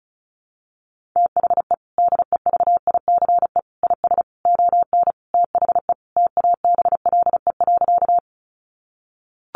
Morse code, heard as a keyed tone, about 34 wpm, 700 Hz. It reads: THE DEVICE IS ON THE TABLE.
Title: Sentences from Top 1500 Words 35wpm